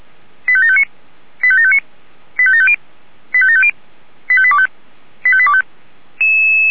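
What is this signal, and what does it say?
Five-tone selective call (ZVEI), heard as quick runs of five tones, tones of about 70 ms. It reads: 85759 85759 85750 85750 85825 85825
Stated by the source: ZVEI five-tone call received by SDR.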